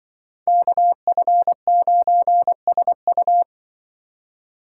Morse code, heard as a keyed tone, about 24 wpm, 700 Hz. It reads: KF9SU
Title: Speed Racing - USA 2x2 Call Signs 20wpm